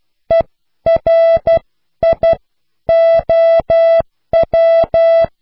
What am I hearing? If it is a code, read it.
Morse code, heard as a keyed tone, about 12 wpm, spaced as for 10 wpm, 660 Hz. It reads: ERIOW